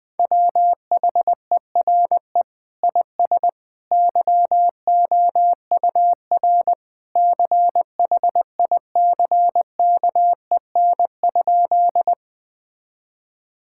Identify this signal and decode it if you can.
Morse code, heard as a keyed tone, about 20 wpm, 700 Hz. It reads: WHERE IS YOUR CHICKEN?
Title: Sentences from Top 1000 Words - Review Part 5 - 20wpm